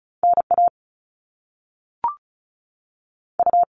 Morse code, heard as a keyed tone, about 36 wpm, 700 Hz. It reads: N A U
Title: Top Two Letter Combinations in Words 35wpm